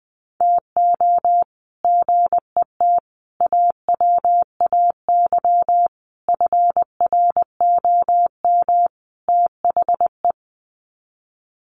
Morse code, heard as a keyed tone, about 20 wpm, 700 Hz. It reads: TO GET AWAY FROM THE
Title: Sets of 5 Words Encore2 20wpm